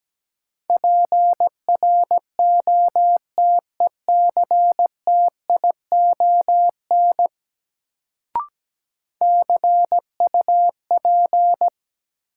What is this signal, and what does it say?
Morse code, heard as a keyed tone, about 17 wpm, 700 Hz. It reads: PROTECTION CUP
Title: Sentences from Top 1600 Words - New Word Drill Alt - 17wpm